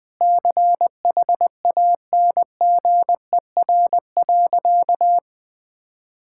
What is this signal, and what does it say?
Morse code, heard as a keyed tone, about 19 wpm, 700 Hz. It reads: CHANGER.
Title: Sentences with 6-Word Phrases - Spoken Beginning - 20wpm